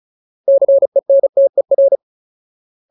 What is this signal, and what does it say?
Morse code, heard as a keyed tone, about 34 wpm, 550 Hz. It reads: CENTER
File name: Ultimate Word List / 35WPM Ultimate